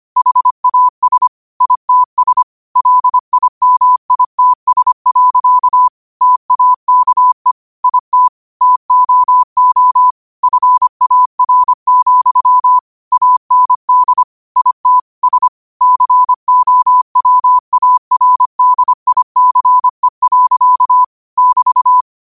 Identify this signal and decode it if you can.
Morse code, heard as a keyed tone, about 25 wpm, 1000 Hz. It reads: HAS ITS LIMITS. TAKE IT TOO FAR, AND IT S COWARDICE. <BT>